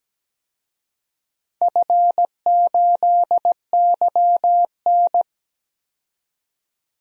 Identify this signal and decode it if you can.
Morse code, heard as a keyed tone, about 17 wpm, 700 Hz. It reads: F8YN
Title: DXCC-100 Call Signs - 4 Characters - 17wpm